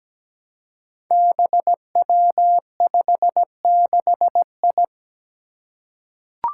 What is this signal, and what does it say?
Morse code, heard as a keyed tone, about 17 wpm, 700 Hz. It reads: BW56I
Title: Five Challenging Letter-Number Combinations 17wpm